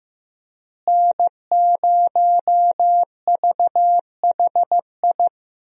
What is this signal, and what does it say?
Morse code, heard as a keyed tone, about 15 wpm, 700 Hz. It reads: N0VHI
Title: USA Call Signs - 5 Characters - Repeated 2x - 15wpm